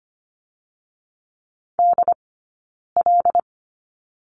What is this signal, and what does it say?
Morse code, heard as a keyed tone, about 25 wpm, 700 Hz. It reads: D L
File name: Lesson 15d - Number 5 - Two Characters 25wpm